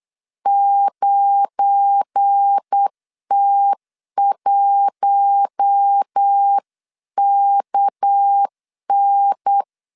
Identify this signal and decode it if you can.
Morse code, heard as a keyed tone, about 8 wpm, 800 Hz. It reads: 9T1KN
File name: LEZIONI / 9 2867 8wpm 3440